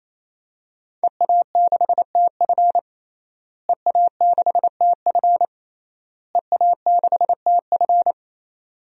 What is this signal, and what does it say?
Morse code, heard as a keyed tone, about 28 wpm, 700 Hz. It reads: EA6TF EA6TF EA6TF
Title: DXCC-100 Call Signs - 5 Characters - Repeated 3x - 28wpm